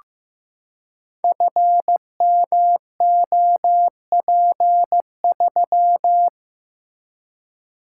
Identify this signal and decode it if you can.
Morse code, heard as a keyed tone, about 15 wpm, 700 Hz. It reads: FMOP3
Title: Code Groups - 5 Letter:Numbers - 15wpm